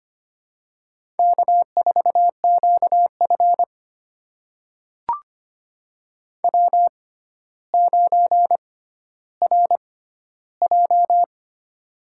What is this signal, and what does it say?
Morse code, heard as a keyed tone, about 25 wpm, spaced as for 13 wpm, 700 Hz. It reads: K4QF W9RJ
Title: USA Call Signs - 4 Characters - 6x Character Spacing - 25wpm